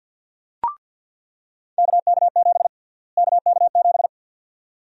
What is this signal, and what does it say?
Morse code, heard as a keyed tone, about 49 wpm, 700 Hz. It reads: KK7 KK7
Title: USA Call Sign Prefixes - Repeated 2x - 50wpm